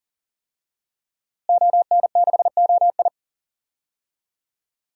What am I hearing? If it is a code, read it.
Morse code, heard as a keyed tone, about 37 wpm, 700 Hz. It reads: ON6OI
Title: DXCC-100 Call Signs - 5 Characters - Repeated 2x - 40wpm